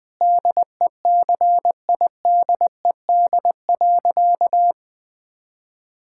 Morse code, heard as a keyed tone, about 20 wpm, 700 Hz. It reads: DECIDED.
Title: Sentences with 3-Word Phrases - 20wpm